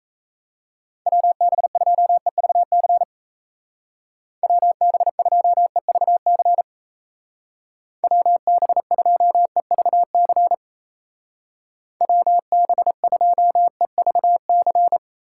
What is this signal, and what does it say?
Morse code, heard as a keyed tone, about 35 wpm, 700 Hz. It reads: WB2EVC WB2EVC WB2EVC WB2EVC